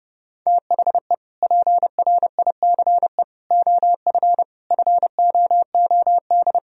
Morse code, heard as a keyed tone, about 30 wpm, 700 Hz. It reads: THE PRICE OF FOOD
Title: Sentences from Top 1000 Words - Review Part 4 - 30wpm